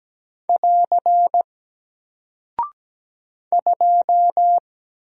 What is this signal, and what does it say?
Morse code, heard as a keyed tone, about 17 wpm, 700 Hz. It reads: <AR> 2
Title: Single Character (Letter:Number:Punctuation:Prosign) 17wpm